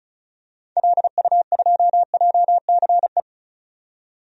Morse code, heard as a keyed tone, about 34 wpm, 700 Hz. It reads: LU2JCE
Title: DXCC-100 Call Signs - 6 Characters - 35wpm